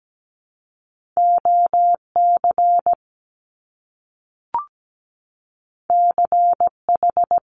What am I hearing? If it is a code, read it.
Morse code, heard as a keyed tone, about 17 wpm, 700 Hz. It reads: OC CH
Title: Top Two Letter Combinations in Words 17wpm